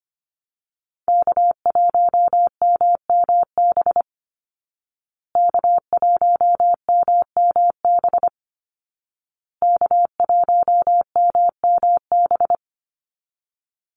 Morse code, heard as a keyed tone, about 25 wpm, 700 Hz. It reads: K1MMB K1MMB K1MMB
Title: Call Signs Repeated 2x - 25wpm